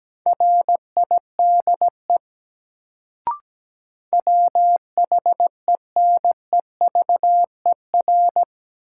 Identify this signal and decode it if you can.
Morse code, heard as a keyed tone, about 17 wpm, 700 Hz. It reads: RIDE WHENEVER
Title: Sentences from Top 1200 Words - New Word Drill Alt - 17wpm